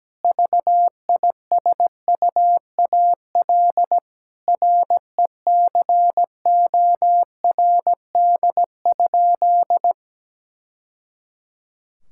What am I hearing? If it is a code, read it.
Morse code, heard as a keyed tone, about 17 wpm, 700 Hz. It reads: VISUAL RECORD?
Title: Sentences from Top 1000 Words - Review Part 6 - 17wpm